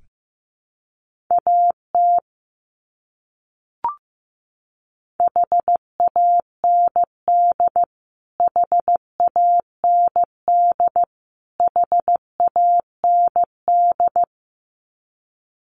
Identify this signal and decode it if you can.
Morse code, heard as a keyed tone, about 15 wpm, 700 Hz. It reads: AT HAND HAND HAND
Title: Top 300 Words Repeated 15wpm